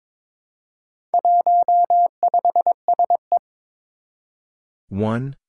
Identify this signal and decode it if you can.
Morse code, heard as a keyed tone, about 22 wpm, 700 Hz. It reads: 15SE
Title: Four Challenging Letter-Number Combinations 22wpm